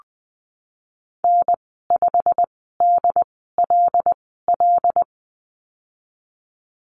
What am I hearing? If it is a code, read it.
Morse code, heard as a keyed tone, about 20 wpm, spaced as for 15 wpm, 700 Hz. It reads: N5DLL
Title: 161 Lesson 26g - Letter g - Call Signs 20wpm